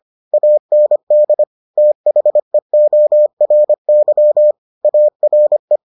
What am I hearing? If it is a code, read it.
Morse code, heard as a keyed tone, about 25 wpm, 600 Hz. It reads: AND THEORY ARE